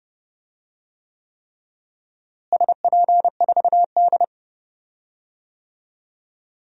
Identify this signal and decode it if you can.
Morse code, heard as a keyed tone, about 30 wpm, 700 Hz. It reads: SP4D
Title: DXCC-100 Call Signs - 4 Characters - Repeated 3x - 30wpm